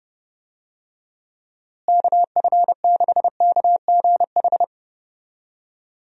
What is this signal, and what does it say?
Morse code, heard as a keyed tone, about 30 wpm, 700 Hz. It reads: KF6KGH
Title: USA Call Signs - 6 Characters - Repeated 3x - 30wpm